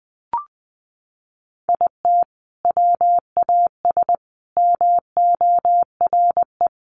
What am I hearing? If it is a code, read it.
Morse code, heard as a keyed tone, about 20 wpm, 700 Hz. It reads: IT WAS MORE
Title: Sentences from Top 1400 Words 20wpm